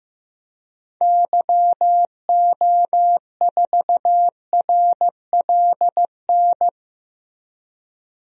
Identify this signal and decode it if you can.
Morse code, heard as a keyed tone, about 15 wpm, 700 Hz. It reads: YO4RLN
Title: DXCC-100 Call Signs - 6 Characters - Repeated 2x - 15wpm